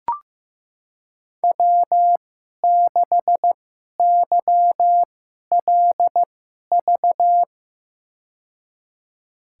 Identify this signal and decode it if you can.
Morse code, heard as a keyed tone, about 15 wpm, spaced as for 11 wpm, 700 Hz. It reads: W6YLV
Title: USA Call Signs - 5 Characters - 2x Character Spacing - 15wpm